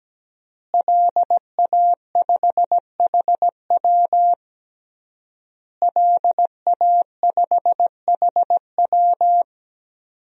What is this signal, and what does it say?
Morse code, heard as a keyed tone, about 17 wpm, 700 Hz. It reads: LA5HW LA5HW
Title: DXCC-100 Call Signs - 5 Characters - Repeated 3x - 17wpm